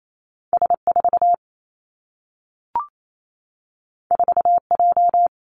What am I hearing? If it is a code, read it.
Morse code, heard as a keyed tone, about 28 wpm, 700 Hz. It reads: S 4 4 J
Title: Two Challenging Letter-Number Combinations 28wpm